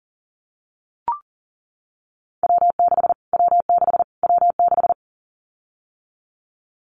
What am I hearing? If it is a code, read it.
Morse code, heard as a keyed tone, about 40 wpm, 700 Hz. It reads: W6 W6 W6
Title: Callsign Prefixes for Top 25 DXCC - Repeated 3x - 40wpm